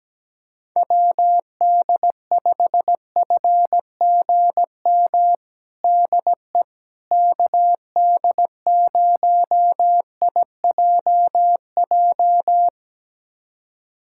Morse code, heard as a keyed tone, about 17 wpm, 700 Hz. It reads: WD5FGM DE KD0IJJ
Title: USA Callsign DE Callsign - 6 Characters - 17wpm